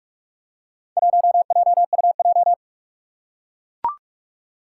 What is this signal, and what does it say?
Morse code, heard as a keyed tone, about 44 wpm, 700 Hz. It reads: 1JUJ
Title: Four Challenging Letter-Number Combinations 45wpm